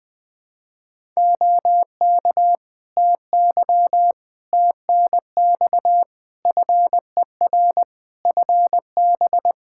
Morse code, heard as a keyed tone, about 20 wpm, 700 Hz. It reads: OK TY TNX FER FB
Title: K1USN SST QSO Practice 20wpm